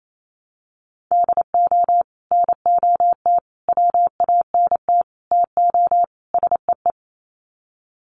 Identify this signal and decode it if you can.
Morse code, heard as a keyed tone, about 28 wpm, 700 Hz. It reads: DO NOT WANT TO SEE
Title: Sets of 5 Words 28wpm